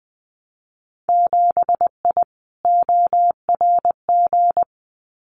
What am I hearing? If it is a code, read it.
Morse code, heard as a keyed tone, about 20 wpm, 700 Hz. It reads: 7I ORG
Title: Winter Field Day Exchanges Encore 20wpm